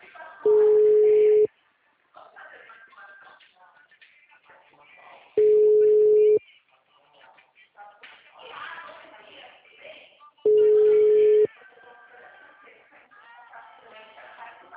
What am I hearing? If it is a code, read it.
Ringback tone (Europe, 425 Hz), with speech.
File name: out- -2013- - - .961